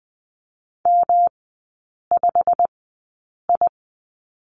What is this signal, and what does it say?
Morse code, heard as a keyed tone, about 20 wpm, spaced as for 8 wpm, 700 Hz. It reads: M5I
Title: 138 Lesson 23e - Letter f - Three Characters 20wpm